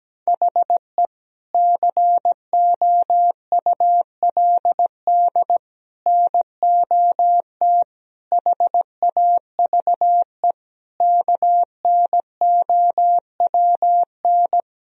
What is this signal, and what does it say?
Morse code, heard as a keyed tone, about 17 wpm, 700 Hz. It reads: HE COULD NOT HAVE KNOWN
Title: Sentences from Top 1000 Words - Review Part 3 - 17wpm